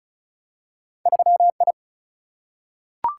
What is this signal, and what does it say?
Morse code, heard as a keyed tone, about 36 wpm, 700 Hz. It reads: 3 I
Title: Single-Letter Single-Number 35wpm